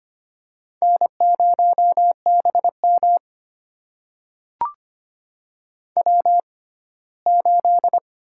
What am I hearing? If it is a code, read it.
Morse code, heard as a keyed tone, about 25 wpm, spaced as for 17 wpm, 700 Hz. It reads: N0BM W8
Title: USA Call Signs - 6x Character Spacing - 25wpm